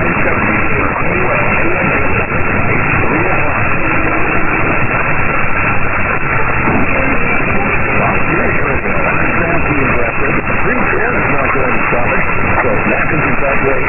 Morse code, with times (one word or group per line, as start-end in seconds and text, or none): none